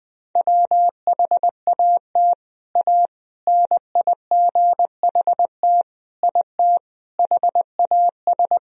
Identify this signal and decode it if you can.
Morse code, heard as a keyed tone, about 20 wpm, 700 Hz. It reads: WHAT A NIGHT IT HAS
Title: Sentences from Top 300 Words and Abbreviations 20wpm